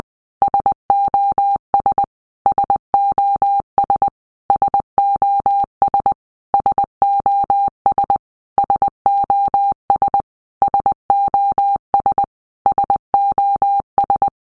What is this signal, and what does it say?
Morse code, heard as a keyed tone, about 20 wpm, 800 Hz. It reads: SOS SOS SOS SOS SOS SOS SOS
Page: SOS sound in Morse Code (Medium)